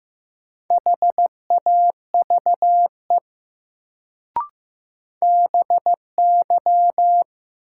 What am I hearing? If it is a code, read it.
Morse code, heard as a keyed tone, about 15 wpm, 700 Hz. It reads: HAVE BY
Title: Sentences from Top 100 Words - New Word Drill - 15wpm